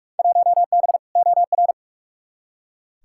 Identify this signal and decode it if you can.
Morse code, heard as a keyed tone, about 42 wpm, 700 Hz. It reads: 1B OR
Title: Field Day Exchanges - 45wpm